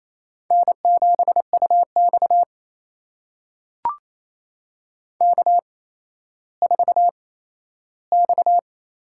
Morse code, heard as a keyed tone, about 28 wpm, 700 Hz. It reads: N 7 U X K 4 X
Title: USA Call Signs - 4 Characters - 8x Character Spacing - 28wpm